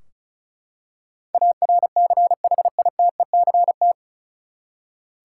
Morse code, heard as a keyed tone, about 36 wpm, 700 Hz. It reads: ARCHITECT
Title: Practice QSO - Exchange 3 - More Info - 35wpm